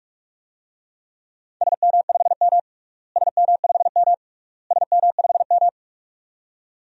Morse code, heard as a keyed tone, about 48 wpm, 700 Hz. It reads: SM5M SM5M SM5M
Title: DXCC-100 Call Signs - 4 Characters - Repeated 3x - 45wpm